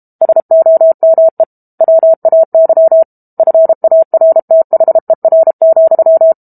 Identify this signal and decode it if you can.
Morse code, heard as a keyed tone, about 32 wpm, 650 Hz. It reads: SOME WAY FARTHER,
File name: Book Two / 06-Chapter 6 - 32 WPM